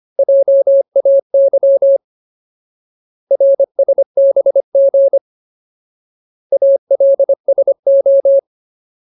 Morse code, heard as a keyed tone, about 25 wpm, 550 Hz. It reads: JAY RSBG ALSO
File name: Ultimate Word List / 25WPM Ultimate